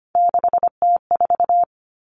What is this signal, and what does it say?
Morse code, heard as a keyed tone, about 25 wpm, 700 Hz. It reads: 6T4
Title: Three Challenging Letter-Number Combinations 25wpm